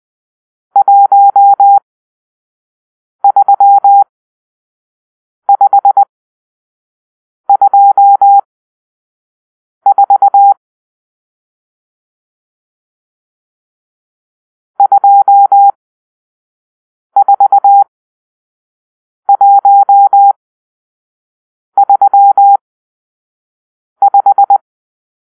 Morse code, heard as a keyed tone, about 20 wpm, spaced as for 5 wpm, 800 Hz. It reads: 13524 24135
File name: arrl / Grp-6 Drill-2